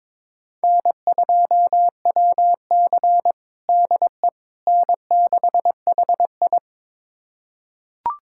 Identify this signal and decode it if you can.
Morse code, heard as a keyed tone, about 22 wpm, 700 Hz. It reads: N2WC DE N6HI
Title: USA Callsign DE Callsign - 4 Characters - 22wpm